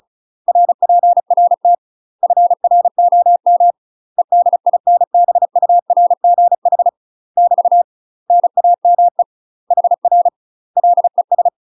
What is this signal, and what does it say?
Morse code, heard as a keyed tone, about 34 wpm, 700 Hz. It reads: RPRT FROM EDINBURGH <BT> NAME HR LES
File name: UK MASTER QSOs TRAINING / 35WPM / UK MASTER QSOs 0001